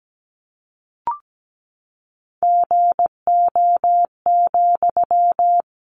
Morse code, heard as a keyed tone, about 17 wpm, 700 Hz. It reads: GO,
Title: Sentences from Top 1400 Words 17wpm